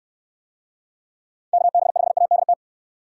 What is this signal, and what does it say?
Morse code, heard as a keyed tone, about 67 wpm, 700 Hz. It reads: KB5ADT